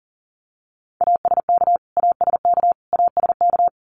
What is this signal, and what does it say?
Morse code, heard as a keyed tone, about 40 wpm, 700 Hz. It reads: ASK ASK ASK
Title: Top 300 Words Repeated 40wpm